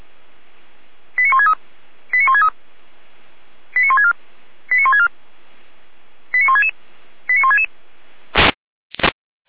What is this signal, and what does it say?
Five-tone selective call (ZVEI), heard as quick runs of five tones, tones of about 80 ms. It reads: 89163 89163 89164 89164 89166 89166